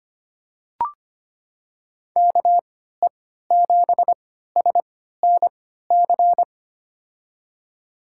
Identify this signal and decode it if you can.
Morse code, heard as a keyed tone, about 25 wpm, spaced as for 14 wpm, 700 Hz. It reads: KE7SNC
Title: USA Call Signs - 3x Character Spacing - 25wpm